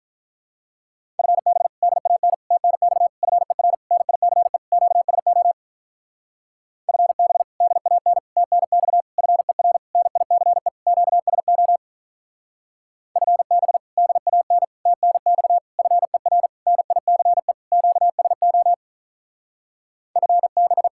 Morse code, heard as a keyed tone, about 45 wpm, 700 Hz. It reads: FB DAN TNX FER NICE QSO FB DAN TNX FER NICE QSO FB DAN TNX FER NICE QSO FB